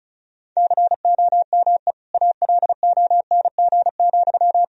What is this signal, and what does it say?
Morse code, heard as a keyed tone, about 36 wpm, 700 Hz. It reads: COME ALONG,
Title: Sentences from Top 1000 Words - Review Part 1 - 35wpm